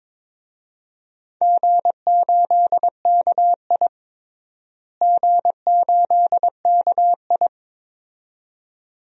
Morse code, heard as a keyed tone, about 22 wpm, 700 Hz. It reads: G8KI G8KI
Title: DXCC-100 Call Signs - 4 Characters - Repeated 2x - 22wpm